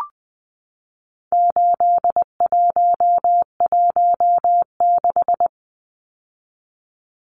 Morse code, heard as a keyed tone, about 20 wpm, 700 Hz. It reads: 8116
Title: Numbers - 4-Digit 20wpm